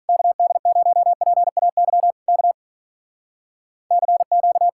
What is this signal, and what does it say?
Morse code, heard as a keyed tone, about 46 wpm, 700 Hz. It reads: KD0PAY K CQ